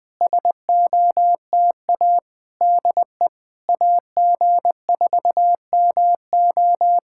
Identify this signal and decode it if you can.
Morse code, heard as a keyed tone, about 20 wpm, 700 Hz. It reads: SOTA DE AG4MO
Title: SOTA QSO Practice 20wpm